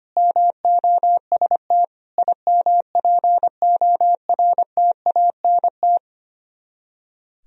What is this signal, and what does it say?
Morse code, heard as a keyed tone, about 30 wpm, 700 Hz. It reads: MOST IMPORTANT
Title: Speed-Racing - Sets of 3 Words 25wpm